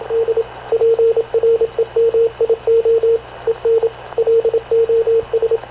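Morse code, heard as a keyed tone, about 26 wpm, 460 Hz, with radio noise. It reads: D PREMIO EN LOS
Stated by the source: Havana Radio / CLA sending Spanish press (PX) in Morse, recorded by Radio Officer Paul Dumesnil onboard CNAV QUEST/CZDO.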